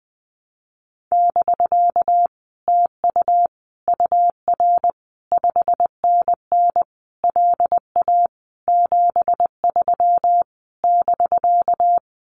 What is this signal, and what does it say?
Morse code, heard as a keyed tone, about 20 wpm, 700 Hz. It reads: <BK> TU UR 5NN LA 73 <BK>
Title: Common QSO Phrases Encore 20wpm